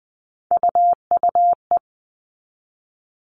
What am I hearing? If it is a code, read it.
Morse code, heard as a keyed tone, about 20 wpm, 700 Hz. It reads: UUE
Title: All Cut Numbers - 3-Digit 20wpm